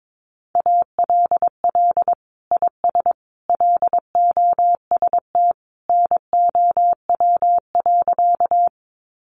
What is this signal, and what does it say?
Morse code, heard as a keyed tone, about 22 wpm, 700 Hz. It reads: ALL IS LOST NOW.
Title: Sentences from Top 1400 Words 22wpm